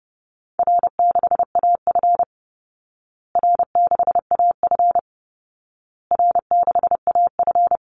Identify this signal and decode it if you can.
Morse code, heard as a keyed tone, about 30 wpm, 700 Hz. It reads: R6AF R6AF R6AF
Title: Call Signs Repeated 2x - 30wpm